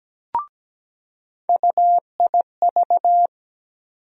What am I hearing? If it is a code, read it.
Morse code, heard as a keyed tone, about 17 wpm, 700 Hz. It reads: UIV
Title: USA Call Sign Postfixes - 3 Letters 17wpm